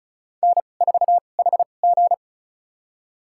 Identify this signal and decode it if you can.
Morse code, heard as a keyed tone, about 36 wpm, spaced as for 25 wpm, 700 Hz. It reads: N4HG
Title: USA Call Signs - 4 Characters - 2x Character Spacing - 35wpm